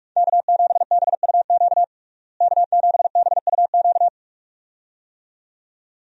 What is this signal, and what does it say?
Morse code, heard as a keyed tone, about 46 wpm, 700 Hz. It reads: K7BUQ K7BUQ
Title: USA Call Signs - 5 Characters - Repeated 2x - 45wpm